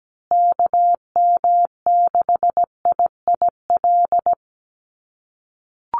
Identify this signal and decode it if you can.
Morse code, heard as a keyed tone, about 17 wpm, 700 Hz. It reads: KM6IIL
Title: USA Call Signs Repeated 2x 17wpm